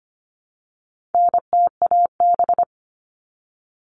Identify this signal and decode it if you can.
Morse code, heard as a keyed tone, about 25 wpm, 700 Hz. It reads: NTAB
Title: All Cut Numbers - 4-Digit 25wpm